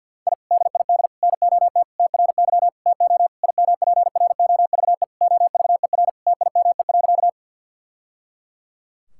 Morse code, heard as a keyed tone, about 50 wpm, 700 Hz. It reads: I DID NOT TRY TO IMPROVE OVER TIME.
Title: Sentences from Top 1000 Words - Review Part 5 - 50wpm